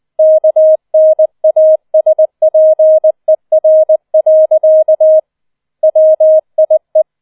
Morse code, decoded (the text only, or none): KNASPER. WIE